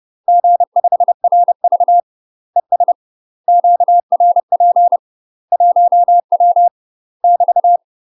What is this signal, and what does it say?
Morse code, heard as a keyed tone, about 30 wpm, 700 Hz. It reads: G5RV ES QRP 1W <BT>
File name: UK MASTER QSOs TRAINING / 30WPM / UK MASTER QSOs 0041